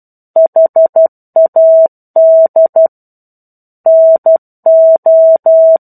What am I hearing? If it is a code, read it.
Morse code, heard as a keyed tone, about 12 wpm, 650 Hz. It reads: HAD NO
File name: 01-Chapter-01 12 WPM